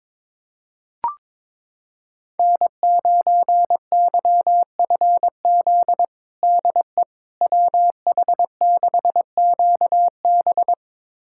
Morse code, heard as a keyed tone, about 22 wpm, 700 Hz. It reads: N9YFZ DE WH6QB
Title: USA Callsign DE Callsign - 5 Characters - 22wpm